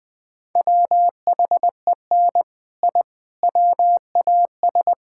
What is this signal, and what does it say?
Morse code, heard as a keyed tone, about 19 wpm, 700 Hz. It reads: WHEN I WAS
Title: Build A Phrase - 5-Word Phrases - 20wpm